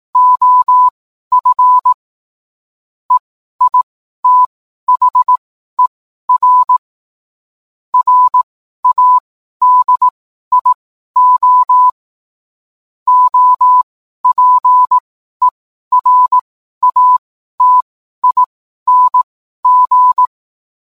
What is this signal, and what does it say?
Morse code, heard as a keyed tone, about 18 wpm, spaced as for 12 wpm, 1000 Hz. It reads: OF EITHER RADIO OPERATING
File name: cwblog / cw mp3 directory / mem morse 12